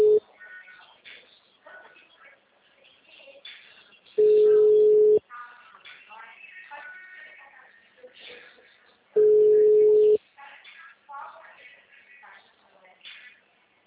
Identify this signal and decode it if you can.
Ringback tone (Europe, 425 Hz), with speech.
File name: out- -2002- - - .80472